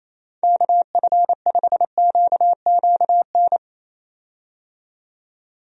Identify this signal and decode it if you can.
Morse code, heard as a keyed tone, about 27 wpm, 700 Hz. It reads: KF5QQN
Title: USA Call Signs - 6 Characters - 28wpm